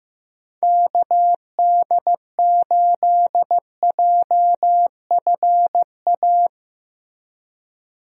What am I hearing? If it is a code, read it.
Morse code, heard as a keyed tone, about 15 wpm, 700 Hz. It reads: KD8JFA
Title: USA Call Signs - 6 Characters - Repeated 3x - 15wpm